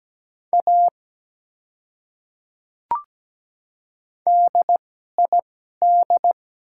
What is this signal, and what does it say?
Morse code, heard as a keyed tone, about 17 wpm, spaced as for 12 wpm, 700 Hz. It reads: A DID
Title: 2x Character Spacing - Top 100 Words - 17wpm